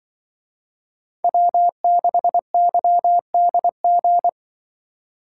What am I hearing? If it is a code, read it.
Morse code, heard as a keyed tone, about 23 wpm, 700 Hz. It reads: W6YDG
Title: Speed Racing - USA 1x3 Call Signs 20wpm